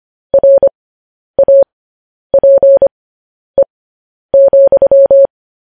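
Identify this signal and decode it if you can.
Morse code, heard as a keyed tone, about 25 wpm, spaced as for 10 wpm, 550 Hz. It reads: RAPE,
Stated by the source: Morse code news headlines sent at 10 words per minute.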